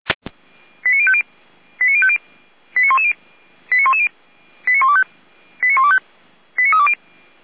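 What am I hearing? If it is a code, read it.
Five-tone selective call (ZVEI), heard as quick runs of five tones, tones of about 70 ms. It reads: 89050 89110 89110 89125 89125 89320
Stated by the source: Five-tone ZVEI selective call 89140, received by SDR, feuerwehr-orlamuende fire brigade.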